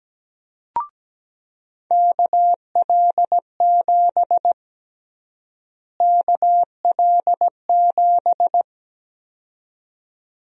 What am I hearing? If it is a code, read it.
Morse code, heard as a keyed tone, about 17 wpm, 700 Hz. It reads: KL7 KL7
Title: USA Call Sign Prefixes - Repeated 2x - 17wpm